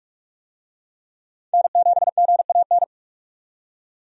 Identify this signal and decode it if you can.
Morse code, heard as a keyed tone, about 43 wpm, 700 Hz. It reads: N7GAN
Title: USA Call Signs - 5 Characters - 45wpm